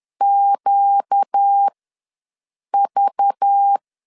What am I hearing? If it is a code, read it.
Morse code, heard as a keyed tone, about 10 wpm, spaced as for 6 wpm, 800 Hz. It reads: QV